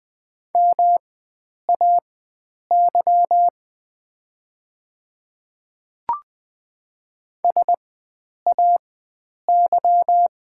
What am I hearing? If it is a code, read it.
Morse code, heard as a keyed tone, about 20 wpm, spaced as for 9 wpm, 700 Hz. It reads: MAY SAY
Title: 4x Character Spacing - Top 500 Words - 20wpm